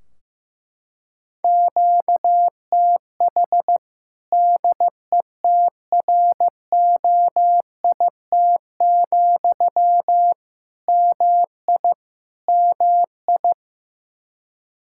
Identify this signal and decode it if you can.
Morse code, heard as a keyed tone, about 15 wpm, 700 Hz. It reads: QTH DETROIT, MI MI
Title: Practice QSO - Exchange 1 - Beginning - 15wpm